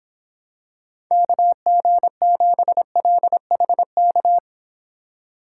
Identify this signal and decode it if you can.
Morse code, heard as a keyed tone, about 26 wpm, 700 Hz. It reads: KG7LHK